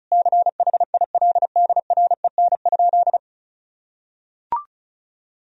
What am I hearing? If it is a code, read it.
Morse code, heard as a keyed tone, about 34 wpm, 700 Hz. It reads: CHILDREN?
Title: Sentences from Top 1000 Words - Review Part 2 - 35wpm